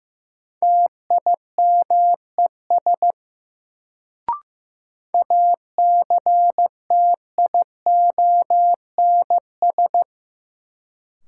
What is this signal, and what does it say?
Morse code, heard as a keyed tone, about 15 wpm, 700 Hz. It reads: TIMES ACTIONS
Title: Sentences from Top 1000 Words - Review Part 6 - New Word Drill - 15wpm